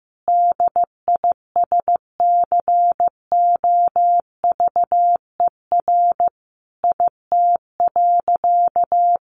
Morse code, heard as a keyed tone, about 15 wpm, 700 Hz. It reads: DISCOVER IT.
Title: Sentences from Top 1100 Words 15wpm